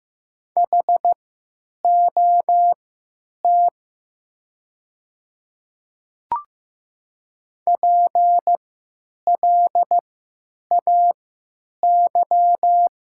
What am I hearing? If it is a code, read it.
Morse code, heard as a keyed tone, about 15 wpm, spaced as for 9 wpm, 700 Hz. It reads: HOT PLAY
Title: 3x Character Spacing - Top 200 Words - 15wpm